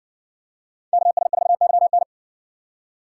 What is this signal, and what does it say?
Morse code, heard as a keyed tone, about 60 wpm, 700 Hz. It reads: KS4YN